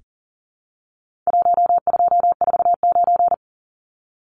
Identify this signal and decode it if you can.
Morse code, heard as a keyed tone, about 40 wpm, 700 Hz. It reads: 1249
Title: Numbers - 4-Digit 40wpm